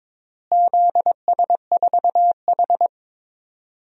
Morse code, heard as a keyed tone, about 21 wpm, 700 Hz. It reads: ZS4H
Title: DXCC-100 Call Signs - 4 Characters - Repeated 3x - 22wpm